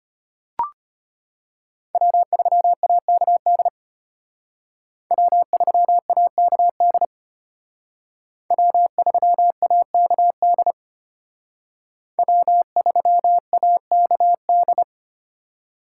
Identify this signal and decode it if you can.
Morse code, heard as a keyed tone, about 30 wpm, 700 Hz. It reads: W3AKD W3AKD W3AKD W3AKD
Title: Speed Racing - USA 1x3 Call Signs 25wpm